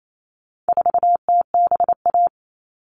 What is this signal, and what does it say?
Morse code, heard as a keyed tone, about 28 wpm, 700 Hz. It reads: 4TBA
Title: All Cut Numbers - 4-Digit 28wpm